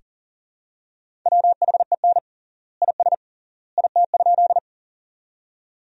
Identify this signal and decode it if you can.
Morse code, heard as a keyed tone, about 40 wpm, 700 Hz. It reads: WHEN IS IT?
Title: 3x Word-Spacing - Sentences from Top 100 Words - 40wpm